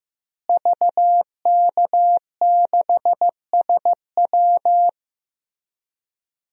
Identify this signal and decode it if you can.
Morse code, heard as a keyed tone, about 15 wpm, 700 Hz. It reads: VK6SW
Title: DXCC-100 Call Signs - 5 Characters - Repeated 2x - 15wpm